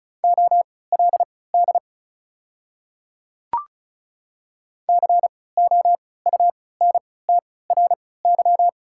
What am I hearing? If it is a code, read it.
Morse code, heard as a keyed tone, about 34 wpm, spaced as for 20 wpm, 690 Hz. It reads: OLD COUNTRY
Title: 3x Character Spacing - Top 300 Words - 35wpm